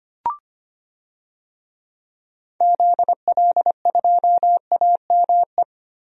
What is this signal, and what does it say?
Morse code, heard as a keyed tone, about 25 wpm, 700 Hz. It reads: ZL2AME
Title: DXCC-100 Call Signs - 6 Characters - Repeated 2x - 25wpm